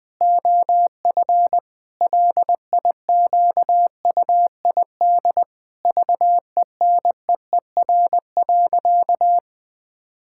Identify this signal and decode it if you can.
Morse code, heard as a keyed tone, about 21 wpm, 700 Hz. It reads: OF LIQUID VENEER.
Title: Pangram Challenge v2 - 20wpm